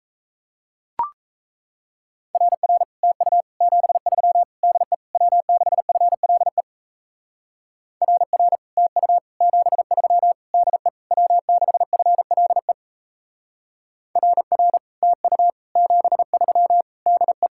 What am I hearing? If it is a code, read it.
Morse code, heard as a keyed tone, about 37 wpm, 700 Hz. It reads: RR TU 73 DE W6FLE RR TU 73 DE W6FLE RR TU 73 DE